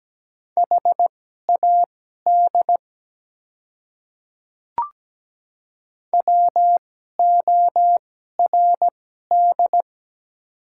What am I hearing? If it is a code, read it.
Morse code, heard as a keyed tone, about 17 wpm, spaced as for 12 wpm, 700 Hz. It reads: HAD WORD
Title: 2x Character Spacing - Top 100 Words - 17wpm